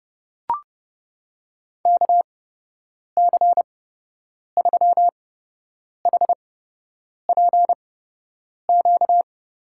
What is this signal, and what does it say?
Morse code, heard as a keyed tone, about 30 wpm, spaced as for 8 wpm, 700 Hz. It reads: KC3HPQ
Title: USA Call Signs - 8x Character Spacing - 30wpm